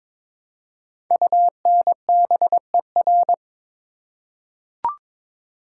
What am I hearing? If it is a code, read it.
Morse code, heard as a keyed tone, about 22 wpm, 700 Hz. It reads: UNBER
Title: Code Groups - 5 Letters - 22wpm